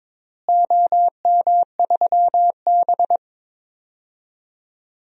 Morse code, heard as a keyed tone, about 22 wpm, 700 Hz. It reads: OM3B
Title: DXCC-100 Call Signs - 4 Characters - Repeated 3x - 22wpm